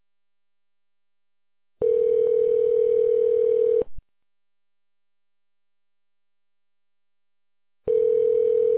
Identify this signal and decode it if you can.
Ringback tone (North America), 440 + 480 Hz.